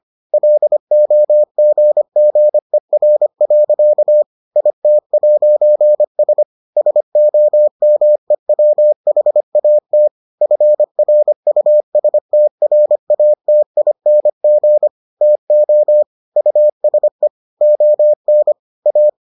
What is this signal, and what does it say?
Morse code, heard as a keyed tone, about 25 wpm, 600 Hz. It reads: LOGGER. IT'S SOMEWHAT FRUSTRATING TO USE ON A